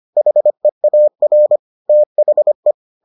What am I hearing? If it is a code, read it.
Morse code, heard as a keyed tone, about 25 wpm, 600 Hz. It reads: HEAR THE